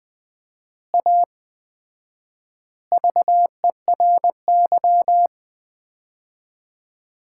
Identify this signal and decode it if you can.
Morse code, heard as a keyed tone, about 20 wpm, 700 Hz. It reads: A VERY
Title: 4x Word-Spacing - Sets of 3 Words - 20wpm